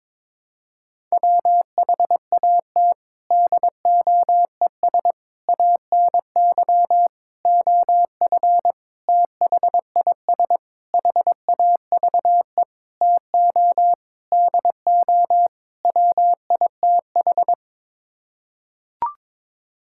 Morse code, heard as a keyed tone, about 22 wpm, 700 Hz. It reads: WHAT DOES ANY OF THIS HAVE TO DO WITH
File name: Sets of 9 Words - 22wpm